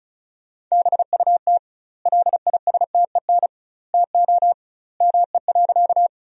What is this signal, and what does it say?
Morse code, heard as a keyed tone, about 36 wpm, 700 Hz. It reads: BUT LISTEN TO ME.
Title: 2x Word-Spacing - Sentences from Top 400 Words - 35wpm